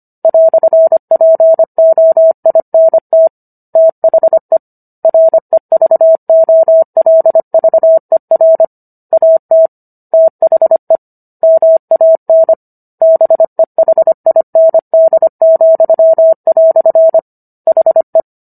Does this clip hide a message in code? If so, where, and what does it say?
Yes: Morse code from 0.2 to 18.2 s, heard as a keyed tone, about 25 wpm, 650 Hz. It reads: "POINT THE REVOLVER AT THE MAN BEHIND," HE